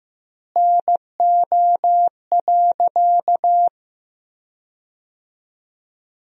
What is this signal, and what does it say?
Morse code, heard as a keyed tone, about 16 wpm, 700 Hz. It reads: NO.
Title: 2x Word-Spacing - Sentences from Top 500 Words - 15wpm